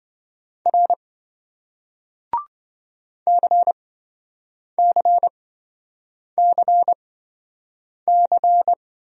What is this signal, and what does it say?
Morse code, heard as a keyed tone, about 26 wpm, spaced as for 7 wpm, 700 Hz. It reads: R CCCC